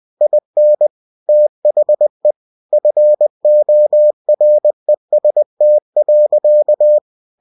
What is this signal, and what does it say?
Morse code, heard as a keyed tone, about 20 wpm, 600 Hz. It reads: IN THE FOREST.
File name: cw-books / winnie-the-pooh / winnie-20wpm-page-0015